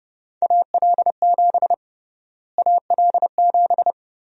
Morse code, heard as a keyed tone, about 30 wpm, 700 Hz. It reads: AL7 AL7
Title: USA Call Sign Prefixes - Repeated 2x - 30wpm